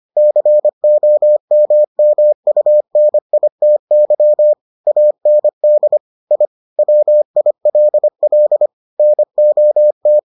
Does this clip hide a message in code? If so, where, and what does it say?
Yes: Morse code from 0.2 to 10.2 s, heard as a keyed tone, about 25 wpm, 600 Hz. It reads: COMMUNITY AND I WILL NOT